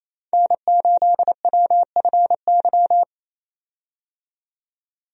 Morse code, heard as a keyed tone, about 28 wpm, 700 Hz. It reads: N8WFY
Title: USA Call Signs - 5 Characters - 28wpm